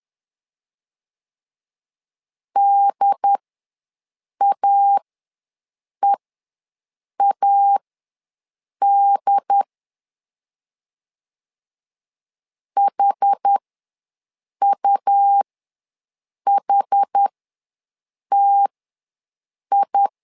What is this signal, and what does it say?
Morse code, heard as a keyed tone, about 10 wpm, spaced as for 6 wpm, 800 Hz. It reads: DAEAD HUHTI
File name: LEZIONI / 2 UHDN / 2 UHD ws 10wpm 2058